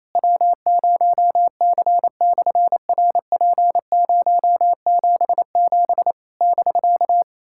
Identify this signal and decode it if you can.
Morse code, heard as a keyed tone, about 27 wpm, 700 Hz. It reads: W0C/RP077 <BK>
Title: SOTA QSO Practice 28wpm